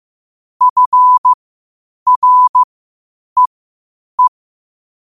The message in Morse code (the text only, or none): FREE